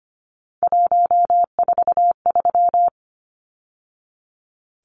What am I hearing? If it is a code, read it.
Morse code, heard as a keyed tone, about 25 wpm, 700 Hz. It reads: 143
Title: Numbers - 3-Digit 25wpm